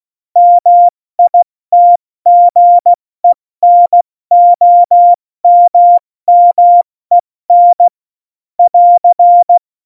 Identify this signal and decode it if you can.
Morse code, heard as a keyed tone, about 16 wpm, spaced as for 14 wpm, 700 Hz. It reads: MITGENOMMEN <AR>
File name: Morsekurs / Witze / Witz 12. Februar 2019, 70 F80